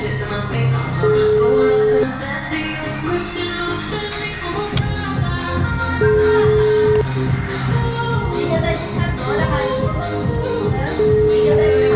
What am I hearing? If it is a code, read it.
Ringback tone (Europe, 425 Hz), with noise.
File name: out- -2009- - - .52466